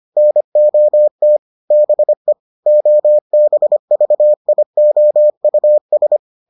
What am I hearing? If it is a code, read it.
Morse code, heard as a keyed tone, about 25 wpm, 600 Hz. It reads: NOT BE OBVIOUS